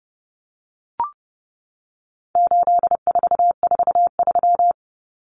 Morse code, heard as a keyed tone, about 30 wpm, 700 Hz. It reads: 8443
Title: Common Cut Numbers - 4-Digit 30wpm